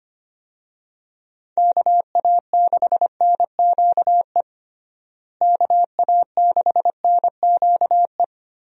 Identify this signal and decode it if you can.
Morse code, heard as a keyed tone, about 25 wpm, 700 Hz. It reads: KA6NQE KA6NQE
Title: USA Call Signs - 6 Characters - Repeated 3x - 25wpm